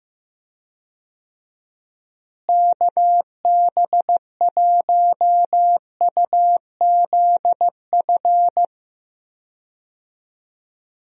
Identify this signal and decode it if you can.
Morse code, heard as a keyed tone, about 16 wpm, 700 Hz. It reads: KB1UZF
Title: USA Call Signs - 6 Characters - Repeated 2x - 15wpm